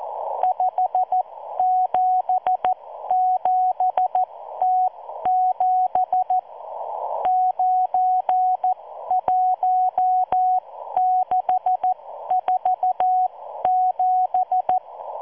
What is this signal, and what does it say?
Morse code, heard as a keyed tone, about 14 wpm, spaced as for 12 wpm, 740 Hz. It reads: 577T7 91647